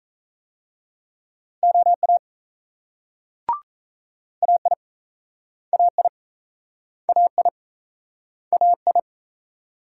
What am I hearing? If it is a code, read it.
Morse code, heard as a keyed tone, about 36 wpm, 700 Hz. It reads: OA AI AI AI AI